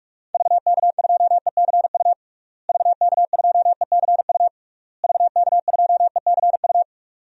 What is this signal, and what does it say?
Morse code, heard as a keyed tone, about 47 wpm, 700 Hz. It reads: VK2ECU VK2ECU VK2ECU
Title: DXCC-100 Call Signs - 6 Characters - Repeated 3x - 45wpm